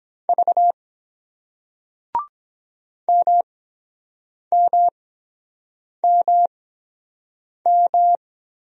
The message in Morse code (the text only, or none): V MMMM